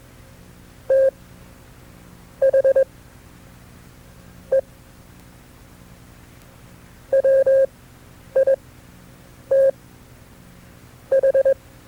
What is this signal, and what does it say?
Morse code, heard as a keyed tone, about 21 wpm, spaced as for 6 wpm, 540 Hz. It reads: THE WITH
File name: FreeFiles / CW-practice / KB6MT-tape2-side2